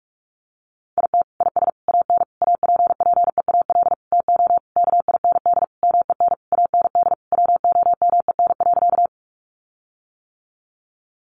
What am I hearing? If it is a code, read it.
Morse code, heard as a keyed tone, about 44 wpm, 700 Hz. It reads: IT IS AN APPEAL TO KIND MEN AND WOMEN.
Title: Sentences from Top 1300 Words 45wpm